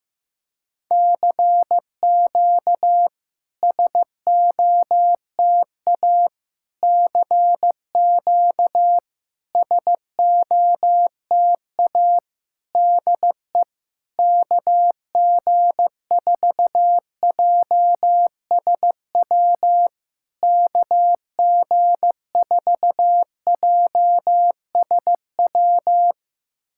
Morse code, heard as a keyed tone, about 15 wpm, 700 Hz. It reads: CQ SOTA CQ SOTA DE KG4JSW KG4JSW
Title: SOTA QSO Practice 15wpm